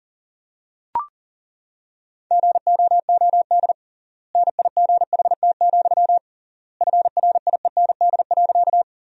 Morse code, heard as a keyed tone, about 37 wpm, 700 Hz. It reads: GOOD NIGHT, FRIEND.
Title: 3x Word-Spacing - Sentences from Top 300 Words - 40wpm